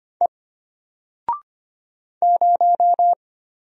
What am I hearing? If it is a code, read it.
Morse code, heard as a keyed tone, about 25 wpm, 700 Hz. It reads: E 0
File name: Single-Letter-Number-25wpm